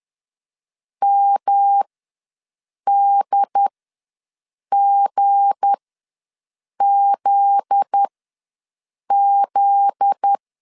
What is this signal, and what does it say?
Morse code, heard as a keyed tone, about 10 wpm, spaced as for 6 wpm, 800 Hz. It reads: MDGZZ
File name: LEZIONI / 7 GF9Z / 7 GF9Z ws 10wpm 3919